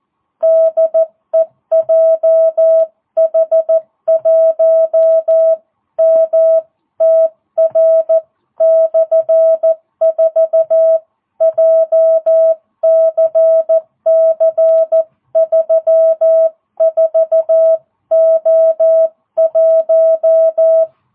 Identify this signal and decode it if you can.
Morse code, heard as a keyed tone, about 14 wpm, 660 Hz. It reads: DE JH1MTR/4 JCC34O1